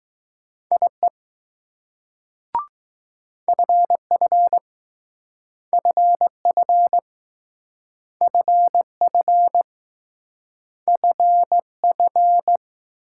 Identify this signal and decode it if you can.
Morse code, heard as a keyed tone, about 19 wpm, 700 Hz. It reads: IE FF FF FF FF